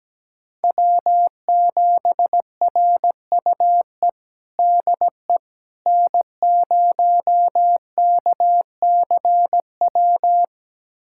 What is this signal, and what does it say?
Morse code, heard as a keyed tone, about 17 wpm, 700 Hz. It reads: W7RUE DE N0KCW
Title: USA Callsign DE Callsign - 5 Characters - 17wpm